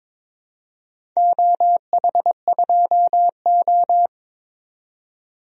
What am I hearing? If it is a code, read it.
Morse code, heard as a keyed tone, about 22 wpm, 700 Hz. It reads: OH2O
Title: DXCC-100 Call Signs - 4 Characters - 22wpm